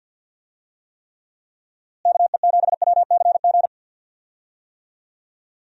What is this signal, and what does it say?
Morse code, heard as a keyed tone, about 52 wpm, 700 Hz. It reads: KE7WCG
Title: USA Call Signs - 6 Characters - Repeated 3x - 50wpm